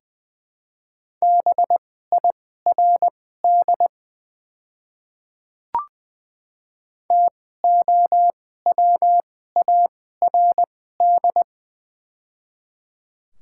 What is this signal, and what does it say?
Morse code, heard as a keyed tone, about 20 wpm, spaced as for 14 wpm, 690 Hz. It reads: BIRD TOWARD
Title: 2x Character Spacing - Top 500 Words - 20wpm